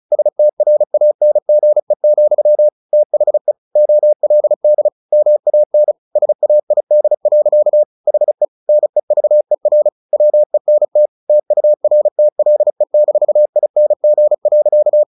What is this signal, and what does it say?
Morse code, heard as a keyed tone, about 34 wpm, 600 Hz. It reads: STRANGE, THE OLD MAN SAID. HE NEVER WENT TURTLE-ING.
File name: the old man and the sea / the old man and the sea 1 35wpm